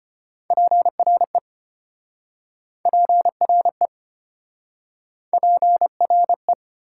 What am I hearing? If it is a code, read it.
Morse code, heard as a keyed tone, about 33 wpm, 700 Hz. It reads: PRE PRE PRE